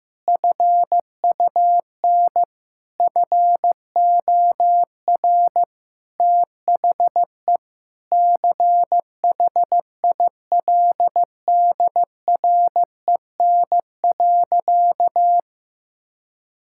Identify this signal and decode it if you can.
Morse code, heard as a keyed tone, about 15 wpm, 700 Hz. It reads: FUN FOR THE CHILDREN.
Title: Sentences from Top 1000 Words - Review Part 5 - 15wpm